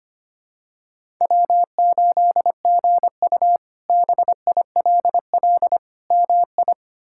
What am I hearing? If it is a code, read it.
Morse code, heard as a keyed tone, about 25 wpm, 700 Hz. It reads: W8GU BILL MI
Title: K1USN SST QSO Practice - 25wpm